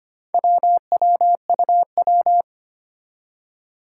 Morse code, heard as a keyed tone, about 25 wpm, 700 Hz. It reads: WWUW
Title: Four Challenging Letter-Number Combinations 25wpm